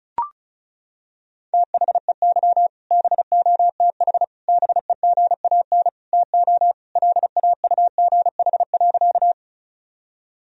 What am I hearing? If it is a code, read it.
Morse code, heard as a keyed tone, about 36 wpm, 700 Hz. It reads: THEY BOTH BEGAN TO LAUGH.
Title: Sentences from Top 1000 Words - Review Part 1 - 35wpm